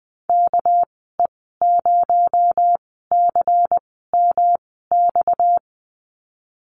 Morse code, heard as a keyed tone, about 20 wpm, spaced as for 14 wpm, 700 Hz. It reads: KE0CMX
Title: 231 Lesson 37g - Letter x - Call Signs 20wpm